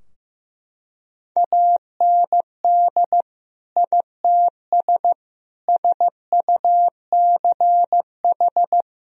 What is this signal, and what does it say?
Morse code, heard as a keyed tone, about 15 wpm, 700 Hz. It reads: AND ITS SUCH
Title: Sentences from Top 1000 Words - Review Part 2 - 15wpm